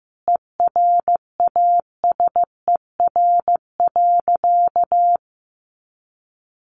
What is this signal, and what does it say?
Morse code, heard as a keyed tone, about 15 wpm, 700 Hz. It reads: ERASER.
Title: Additional Easy Phrases 15wpm